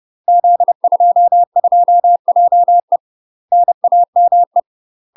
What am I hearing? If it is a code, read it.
Morse code, heard as a keyed tone, about 30 wpm, 700 Hz. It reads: Z22JE NAME
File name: UK MASTER QSOs TRAINING / 30WPM / UK MASTER QSOs 0024